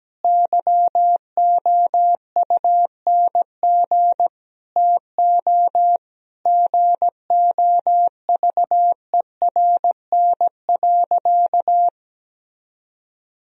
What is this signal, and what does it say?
Morse code, heard as a keyed tone, about 17 wpm, 700 Hz. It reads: YOUNG TO GOVERN.
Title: Sentences from Top 1000 Words - Review Part 2 - 17wpm